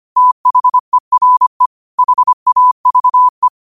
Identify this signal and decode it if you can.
Morse code, heard as a keyed tone, about 25 wpm, 1000 Hz. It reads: THERE HAVE
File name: qotdfiles / 25wpm